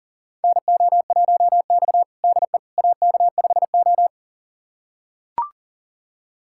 Morse code, heard as a keyed tone, about 40 wpm, 700 Hz. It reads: NO1X DE AK5O
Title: USA Callsign DE Callsign - 4 Characters - 40wpm